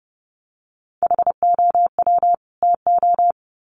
Morse code, heard as a keyed tone, about 30 wpm, 700 Hz. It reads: HOW TO
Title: Sets of 2 Words 30wpm